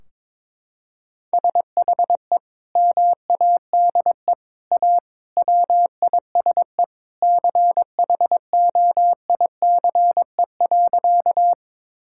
Morse code, heard as a keyed tone, about 22 wpm, 700 Hz. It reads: SHE MADE A WISE CHOICE.
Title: Sentences from Top 1600 Words 22wpm